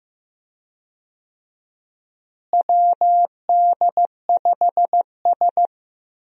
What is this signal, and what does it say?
Morse code, heard as a keyed tone, about 16 wpm, 700 Hz. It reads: WD5S
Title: USA Call Signs - 4 Characters - Repeated 2x - 15wpm